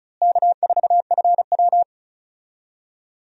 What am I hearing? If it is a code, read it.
Morse code, heard as a keyed tone, about 34 wpm, 700 Hz. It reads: K4FW
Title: USA Call Signs - 4 Characters - Repeated 2x - 35wpm